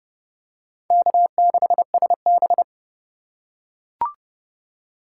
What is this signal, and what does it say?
Morse code, heard as a keyed tone, about 31 wpm, 700 Hz. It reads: K6SB
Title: USA Call Signs - 3x Character Spacing - 30wpm